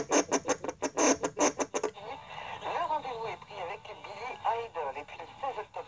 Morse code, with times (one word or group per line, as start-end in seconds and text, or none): none